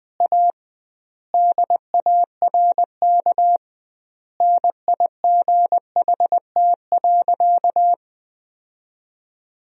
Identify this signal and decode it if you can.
Morse code, heard as a keyed tone, about 20 wpm, 700 Hz. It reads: A DARK NIGHT.
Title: 2x Word-Spacing - Sentences from Top 500 Words - 20wpm